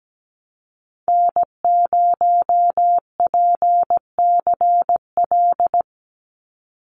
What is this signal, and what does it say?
Morse code, heard as a keyed tone, about 17 wpm, 700 Hz. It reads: N0PCL
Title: NA SOTA Activator Callsigns 17wpm